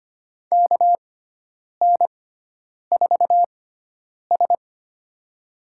Morse code, heard as a keyed tone, about 25 wpm, spaced as for 9 wpm, 700 Hz. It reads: KN4S
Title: USA Call Signs - 6x Character Spacing - 25wpm